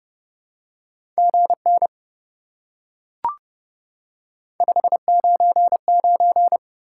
Morse code, heard as a keyed tone, about 28 wpm, 700 Hz. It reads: GN 599